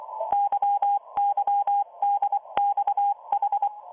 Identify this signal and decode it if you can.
Morse code, heard as a keyed tone, about 24 wpm, 810 Hz. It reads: YYDXH